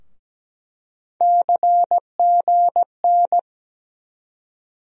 Morse code, heard as a keyed tone, about 17 wpm, 700 Hz. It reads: CGN
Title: USA Call Sign Postfixes - 3 Letters 17wpm